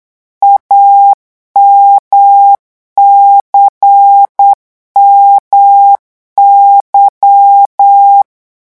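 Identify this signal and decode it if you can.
Morse code, heard as a keyed tone, about 8 wpm, 800 Hz. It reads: AMCMY
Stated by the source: Morse code revision lesson at 8 words per minute, from a French amateur radio course.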